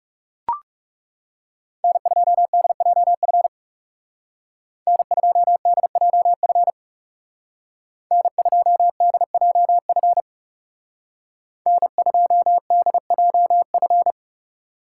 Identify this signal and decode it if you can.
Morse code, heard as a keyed tone, about 37 wpm, 700 Hz. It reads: N2DJF N2DJF N2DJF N2DJF
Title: Speed Racing - USA Call Signs 30wpm